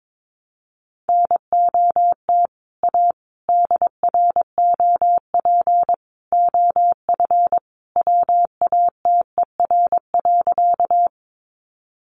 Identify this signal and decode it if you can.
Morse code, heard as a keyed tone, about 22 wpm, 700 Hz. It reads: NOT A DROP OF WATER.
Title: Sentences from Top 1400 Words 22wpm